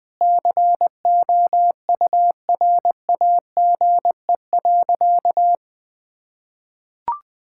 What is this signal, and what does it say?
Morse code, heard as a keyed tone, about 20 wpm, 700 Hz. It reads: COURAGE.
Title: Sentences with 4-Word Phrases - 20wpm